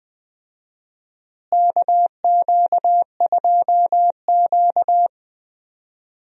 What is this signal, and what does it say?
Morse code, heard as a keyed tone, about 20 wpm, 700 Hz. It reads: KQ2Q
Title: USA Call Signs - 4 Characters - Repeated 2x - 20wpm